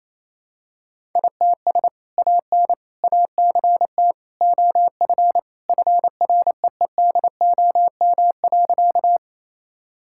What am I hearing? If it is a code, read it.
Morse code, heard as a keyed tone, about 28 wpm, 700 Hz. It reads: ITS AN ACT OF FREEDOM.
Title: Sentences from Top 1000 Words - Review Part 5 - 28wpm